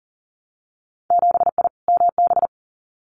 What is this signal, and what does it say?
Morse code, heard as a keyed tone, about 42 wpm, 700 Hz. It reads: 7I MB
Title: Winter Field Day Exchanges Encore 40wpm